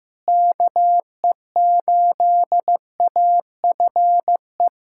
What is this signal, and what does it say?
Morse code, heard as a keyed tone, about 15 wpm, 700 Hz. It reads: KE8AFE
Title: USA Callsign DE Callsign - 6 Characters - 15wpm